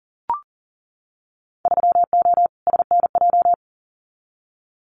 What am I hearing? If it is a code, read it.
Morse code, heard as a keyed tone, about 40 wpm, 700 Hz. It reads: 3O SNJ
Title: Winter Field Day Exchanges Encore 40wpm